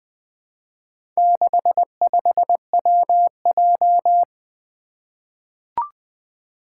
Morse code, heard as a keyed tone, about 20 wpm, 700 Hz. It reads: 65WJ
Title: Four Challenging Letter-Number Combinations 20wpm